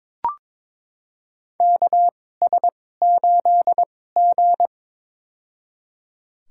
Morse code, heard as a keyed tone, about 22 wpm, spaced as for 16 wpm, 700 Hz. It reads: KS8G
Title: USA Call Signs - 4 Characters - 2x Character Spacing - 22wpm